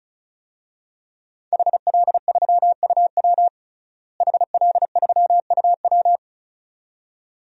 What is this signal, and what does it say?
Morse code, heard as a keyed tone, about 36 wpm, 700 Hz. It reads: HL3UW HL3UW
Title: DXCC-100 Call Signs - 5 Characters - Repeated 2x - 35wpm